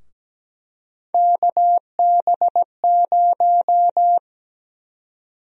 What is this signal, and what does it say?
Morse code, heard as a keyed tone, about 17 wpm, 700 Hz. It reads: KB0
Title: USA Call Sign Prefixes - Repeated 2x - 17wpm